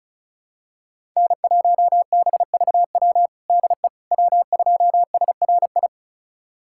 Morse code, heard as a keyed tone, about 36 wpm, 690 Hz. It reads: N1BVW DE W2SRI
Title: USA Callsign DE Callsign - 5 Characters - 35wpm